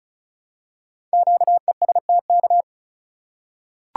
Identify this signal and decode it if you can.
Morse code, heard as a keyed tone, about 34 wpm, 700 Hz. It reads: QESTK
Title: Code Groups - 5 Letters - 35wpm